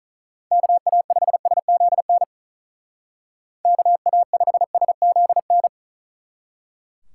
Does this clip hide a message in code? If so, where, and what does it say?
Yes: Morse code from 0.5 to 5.7 s, heard as a keyed tone, about 44 wpm, 700 Hz. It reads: KA5SZN KA5SZN